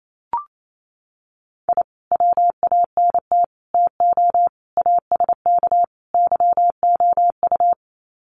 Morse code, heard as a keyed tone, about 28 wpm, 700 Hz. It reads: I WANT TO ASK YOU
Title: Sets of 5 Words 28wpm